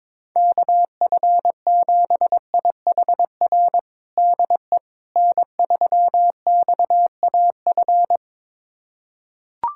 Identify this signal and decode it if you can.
Morse code, heard as a keyed tone, about 22 wpm, 700 Hz. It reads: KF7IHR DE N3XAF
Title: USA Callsign DE Callsign 22wpm